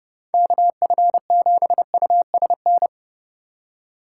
Morse code, heard as a keyed tone, about 37 wpm, 700 Hz. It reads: KF7USN
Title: Speed Racing - USA 2x3 Call Signs 30wpm